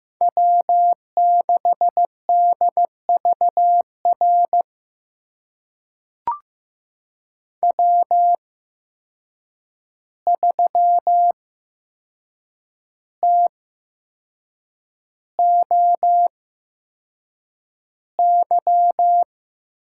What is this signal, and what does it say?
Morse code, heard as a keyed tone, about 15 wpm, spaced as for 6 wpm, 700 Hz. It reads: W6DVR W3TOY
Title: USA Call Signs - 5 Characters - 8x Character Spacing - 15wpm